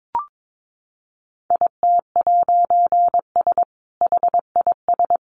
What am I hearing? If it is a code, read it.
Morse code, heard as a keyed tone, about 22 wpm, 700 Hz. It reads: IT'S HIS
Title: A Sound Of Thunder 22wpm